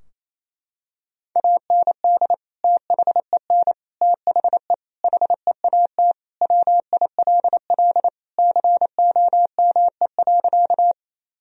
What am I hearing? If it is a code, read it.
Morse code, heard as a keyed tone, about 28 wpm, 700 Hz. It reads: AND THEN THE HEAT WILL COME.
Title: Sentences from Top 1000 Words - Review Part 3 - 28wpm